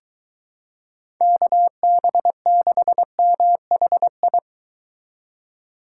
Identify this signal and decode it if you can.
Morse code, heard as a keyed tone, about 23 wpm, 700 Hz. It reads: KB6MHI